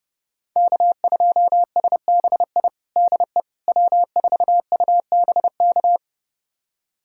Morse code, heard as a keyed tone, about 30 wpm, 700 Hz. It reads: K2SBI DE W4UBK
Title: USA Callsign DE Callsign - 5 Characters - 30wpm